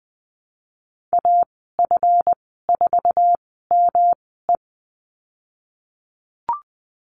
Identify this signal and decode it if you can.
Morse code, heard as a keyed tone, about 19 wpm, spaced as for 14 wpm, 700 Hz. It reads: AF4ME
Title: 147 Lesson 24g - Letter y - Call Signs 20wpm